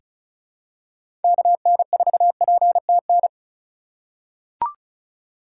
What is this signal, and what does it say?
Morse code, heard as a keyed tone, about 34 wpm, 700 Hz. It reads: KN4PTN
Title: USA Call Signs - 6 Characters - 3x Character Spacing - 35wpm